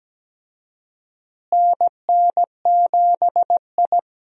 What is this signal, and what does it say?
Morse code, heard as a keyed tone, about 17 wpm, 700 Hz. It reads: NN7I
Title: USA Call Signs - 4 Characters - Repeated 2x - 17wpm